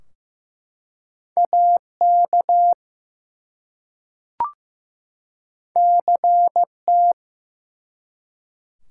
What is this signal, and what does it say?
Morse code, heard as a keyed tone, about 15 wpm, 700 Hz. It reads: AK CT
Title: US States and Canadian Provinces 15wpm